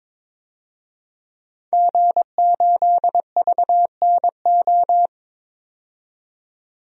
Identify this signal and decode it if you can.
Morse code, heard as a keyed tone, about 22 wpm, 700 Hz. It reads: G8VNO
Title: DXCC-100 Call Signs - 5 Characters - 22wpm